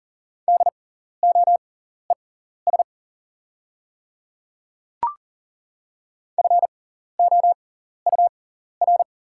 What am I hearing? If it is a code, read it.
Morse code, heard as a keyed tone, about 40 wpm, spaced as for 14 wpm, 700 Hz. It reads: DOES FOUR
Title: 6x Character Spacing - Top 300 Words - 40wpm